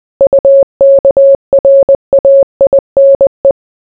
Morse code, heard as a keyed tone, about 20 wpm, 550 Hz. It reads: UKRAINE